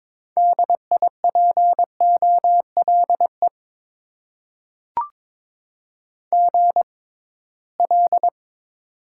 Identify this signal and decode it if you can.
Morse code, heard as a keyed tone, about 21 wpm, spaced as for 16 wpm, 700 Hz. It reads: DIPOLE GL
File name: Top 90 QSO Elements - 6x Character Spacing - 22wpm